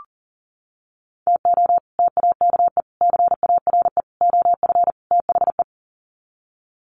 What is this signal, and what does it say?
Morse code, heard as a keyed tone, about 40 wpm, 700 Hz. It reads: TO TAKE CARE OF THE
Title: Sets of 5 Words Encore 40wpm